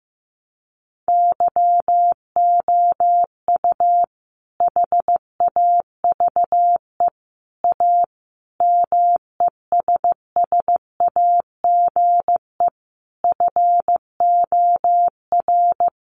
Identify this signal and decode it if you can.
Morse code, heard as a keyed tone, about 14 wpm, 700 Hz. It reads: YOU HAVE A MESSAGE FOR
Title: Sentences from Top 1100 Words 15wpm